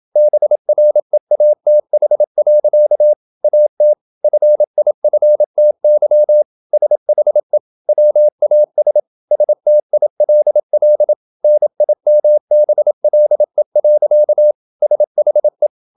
Morse code, heard as a keyed tone, about 27 wpm, 600 Hz. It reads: BREATH. AT FIFTY SHE WAS STILL NIMBLE. SHE
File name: Never 0027wpm